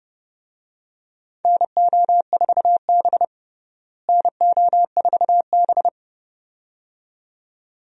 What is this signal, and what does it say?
Morse code, heard as a keyed tone, about 30 wpm, 700 Hz. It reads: NO4B NO4B
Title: USA Call Signs - 4 Characters - Repeated 2x - 30wpm